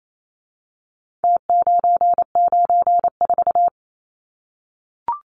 Morse code, heard as a keyed tone, about 28 wpm, 700 Hz. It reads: T994
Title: Numbers - Contest Style - 4-Digit 28wpm